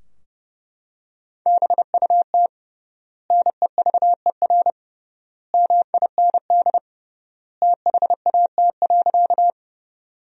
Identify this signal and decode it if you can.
Morse code, heard as a keyed tone, about 32 wpm, 700 Hz. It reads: BUT NEVER MIND THAT.
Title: 3x Word-Spacing - Sentences from Top 500 Words - 30wpm